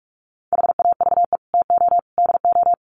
Morse code, heard as a keyed tone, about 44 wpm, 700 Hz. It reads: HAVE TO DO
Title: Sets of 3 Words 45wpm